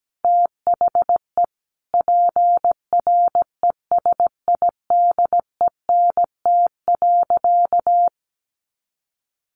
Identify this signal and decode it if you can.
Morse code, heard as a keyed tone, about 17 wpm, 700 Hz. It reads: THE PRESIDENT.
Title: Sentences from Top 1500 Words 17wpm